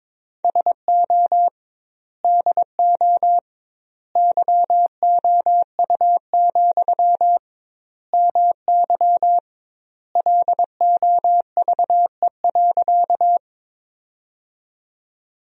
Morse code, heard as a keyed tone, about 22 wpm, 700 Hz. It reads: SO DO YOU, MY LOVE.
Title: 2x Word-Spacing - Sentences from Top 400 Words - 22wpm